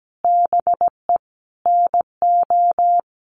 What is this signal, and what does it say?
Morse code, heard as a keyed tone, about 17 wpm, 700 Hz. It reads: BE NO
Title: Sentences from Top 1500 Words 17wpm